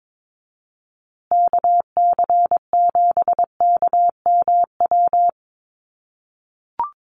Morse code, heard as a keyed tone, about 22 wpm, 700 Hz. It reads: KC7KMW
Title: USA Call Signs Repeated 3x 22wpm